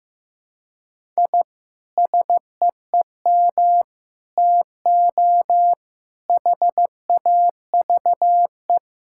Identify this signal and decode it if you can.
Morse code, heard as a keyed tone, about 15 wpm, 700 Hz. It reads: I SEEM TO HAVE
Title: Sentences from Top 1000 Words - Review Part 5 - 15wpm